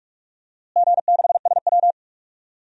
Morse code, heard as a keyed tone, about 42 wpm, 700 Hz. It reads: G6SW
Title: DXCC-100 Call Signs - 4 Characters - Repeated 2x - 45wpm